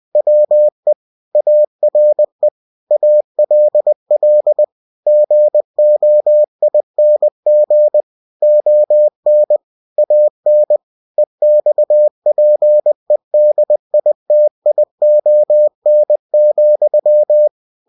Morse code, heard as a keyed tone, about 20 wpm, 600 Hz. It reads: WE ARE ALL GOING ON AN EXPEDITION,
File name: cw-books / winnie-the-pooh / winnie-20wpm-page-0076